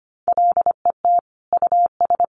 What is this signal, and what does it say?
Morse code, heard as a keyed tone, about 25 wpm, 700 Hz. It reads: LET US
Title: Sentences from Top 1300 Words 25wpm